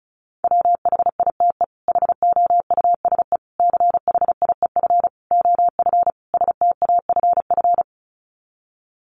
Morse code, heard as a keyed tone, about 34 wpm, 700 Hz. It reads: WHITE HOUSE CHIEF OF STAFF
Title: Sets of 5 Words Encore 35wpm